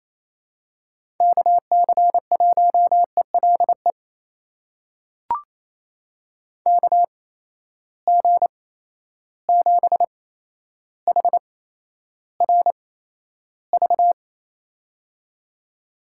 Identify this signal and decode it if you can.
Morse code, heard as a keyed tone, about 28 wpm, spaced as for 12 wpm, 700 Hz. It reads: KC1ELE KG7HRV
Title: USA Call Signs - 6 Characters - 8x Character Spacing - 28wpm